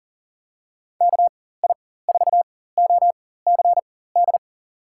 Morse code, heard as a keyed tone, about 40 wpm, spaced as for 19 wpm, 700 Hz. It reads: KI4OCD
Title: USA Call Signs - 6 Characters - 4x Character Spacing - 40wpm